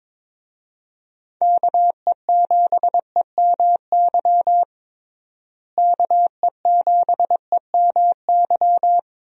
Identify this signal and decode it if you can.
Morse code, heard as a keyed tone, about 22 wpm, 700 Hz. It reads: KE7EMY KE7EMY
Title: USA Call Signs - 6 Characters - Repeated 3x - 22wpm